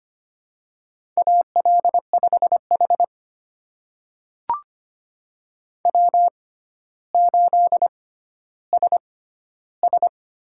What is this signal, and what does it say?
Morse code, heard as a keyed tone, about 25 wpm, spaced as for 13 wpm, 700 Hz. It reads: AL5H W8SS
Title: USA Call Signs - 4 Characters - 6x Character Spacing - 25wpm